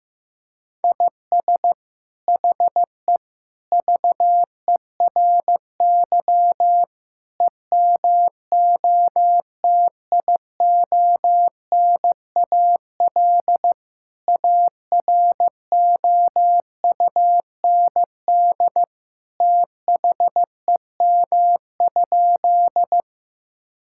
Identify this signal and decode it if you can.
Morse code, heard as a keyed tone, about 15 wpm, 700 Hz. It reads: IS HE VERY EMOTIONAL AROUND THEM?
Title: Sentences from Top 1000 Words - Review Part 5 - 15wpm